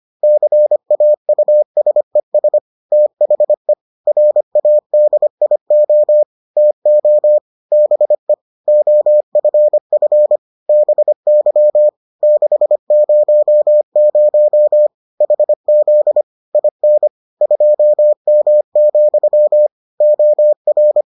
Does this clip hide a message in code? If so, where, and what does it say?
Yes: Morse code from 0.2 to 21.0 s, heard as a keyed tone, about 25 wpm, 600 Hz. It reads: CAUSES THE RADIO TO BE OFF BY 600 HZ IN 2M, OR